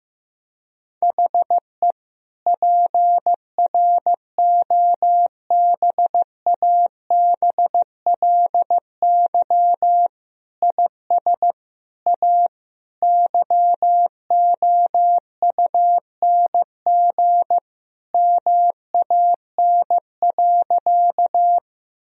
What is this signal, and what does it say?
Morse code, heard as a keyed tone, about 15 wpm, 700 Hz. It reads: HE PROBABLY IS A YOUNG MAN.
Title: Sentences from Top 1000 Words - Review Part 4 - 15wpm